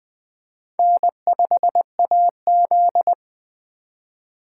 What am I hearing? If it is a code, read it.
Morse code, heard as a keyed tone, about 20 wpm, 700 Hz. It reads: N5AZ
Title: Speed Racing - USA 1x2 Call Signs 17wpm